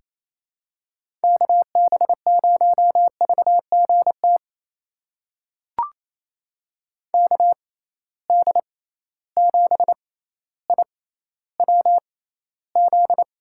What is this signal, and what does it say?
Morse code, heard as a keyed tone, about 28 wpm, spaced as for 14 wpm, 700 Hz. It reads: KB0VGT KD7IWZ
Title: USA Call Signs - 6 Characters - 6x Character Spacing - 28wpm